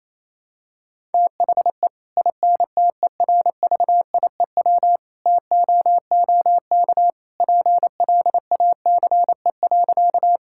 Morse code, heard as a keyed tone, about 28 wpm, 700 Hz. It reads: THE INTERVIEW TOOK PLACE.
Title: Sentences from Top 1600 Words 28wpm